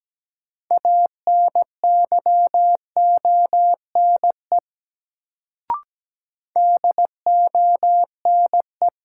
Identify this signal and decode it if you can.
Morse code, heard as a keyed tone, about 17 wpm, 700 Hz. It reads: ANYONE DONE
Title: Sets of 5 Words Encore - New Word Drill - 17wpm